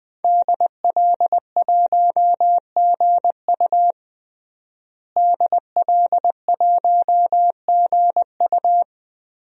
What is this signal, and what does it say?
Morse code, heard as a keyed tone, about 20 wpm, 700 Hz. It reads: DL1GU DL1GU
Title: DXCC-100 Call Signs - 5 Characters - Repeated 2x - 20wpm